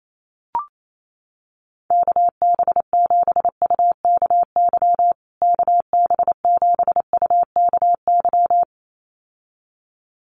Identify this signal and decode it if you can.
Morse code, heard as a keyed tone, about 28 wpm, 700 Hz. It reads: KB7UKY KB7UKY
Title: USA Call Signs Repeated 2x 28wpm